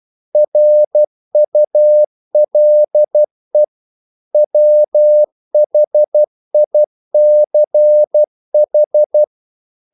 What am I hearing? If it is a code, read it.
Morse code, heard as a keyed tone, about 12 wpm, 600 Hz. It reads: RULE WHICH
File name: CW / Practice / EnglishQuotations0112